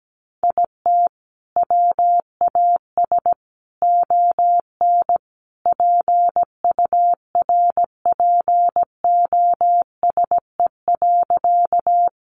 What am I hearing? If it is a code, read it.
Morse code, heard as a keyed tone, about 17 wpm, 700 Hz. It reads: IT WAS ON PURPOSE.
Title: Sentences from Top 800 Words 17wpm